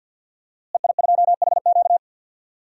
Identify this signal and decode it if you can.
Morse code, heard as a keyed tone, about 47 wpm, 700 Hz. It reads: EI2HQ
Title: DXCC-100 Call Signs - 5 Characters - Repeated 3x - 50wpm